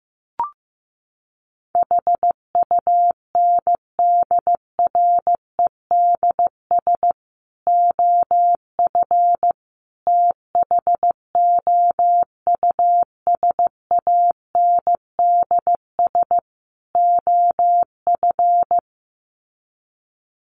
Morse code, heard as a keyed tone, about 15 wpm, 700 Hz. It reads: HUNDREDS OF THOUSANDS OF
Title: Sets of 4 Words 15wpm